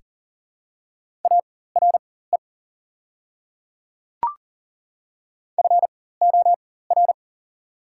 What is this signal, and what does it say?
Morse code, heard as a keyed tone, about 40 wpm, spaced as for 19 wpm, 690 Hz. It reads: ARE FOR
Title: 4x Character Spacing - Top 100 Words - 40wpm